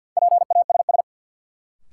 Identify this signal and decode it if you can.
Morse code, heard as a keyed tone, about 46 wpm, 700 Hz. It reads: PASS
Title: Sentences from Top 400 Words - New Word Drill - 50wpm